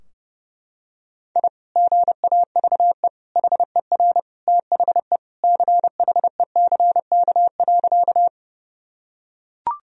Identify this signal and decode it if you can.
Morse code, heard as a keyed tone, about 30 wpm, 700 Hz. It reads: I GAVE HER THE CHECK.
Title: Sentences from Top 1000 Words - Review Part 3 - 30wpm